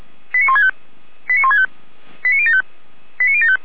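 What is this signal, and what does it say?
Five-tone selective call (ZVEI), heard as quick runs of five tones, tones of about 70 ms. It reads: 89165 89165 89075 89075
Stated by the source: ZVEI five-tone selective call received via SDR.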